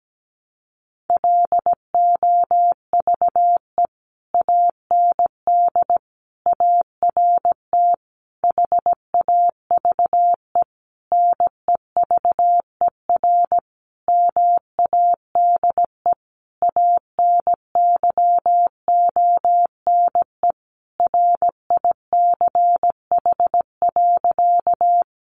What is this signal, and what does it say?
Morse code, heard as a keyed tone, about 17 wpm, 700 Hz. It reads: LOVE AND ART HAVE NEVER MADE ANYONE RICH.
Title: Sentences from Top 1100 Words 17wpm